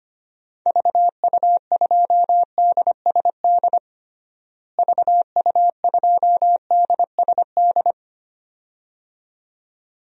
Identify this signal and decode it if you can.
Morse code, heard as a keyed tone, about 25 wpm, 700 Hz. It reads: VU2DSD VU2DSD
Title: DXCC-100 Call Signs - 6 Characters - Repeated 2x - 25wpm